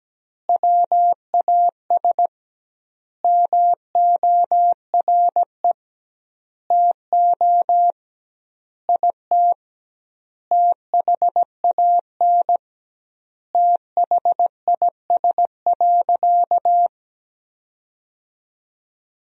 Morse code, heard as a keyed tone, about 17 wpm, 700 Hz. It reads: WAS MORE TO IT THAN THIS.
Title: 2x Word-Spacing - Sentences from Top 100 Words - 17wpm